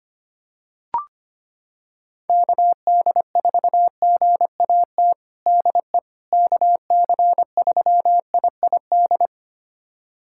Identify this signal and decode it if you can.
Morse code, heard as a keyed tone, about 25 wpm, 700 Hz. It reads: KD4GAT DE KC3IID
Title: USA Callsign DE Callsign - 6 Characters - 25wpm